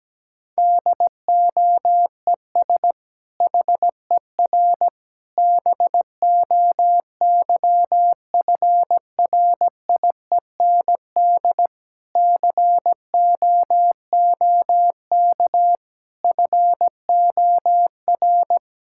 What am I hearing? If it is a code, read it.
Morse code, heard as a keyed tone, about 17 wpm, 700 Hz. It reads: DOES HER BOYFRIEND COOK FOR
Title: Sentences from Top 1000 Words - Review Part 5 - 17wpm